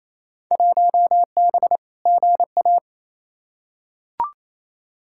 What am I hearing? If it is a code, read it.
Morse code, heard as a keyed tone, about 28 wpm, 700 Hz. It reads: 1B GA
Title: Field Day Exchanges Encore - 28wpm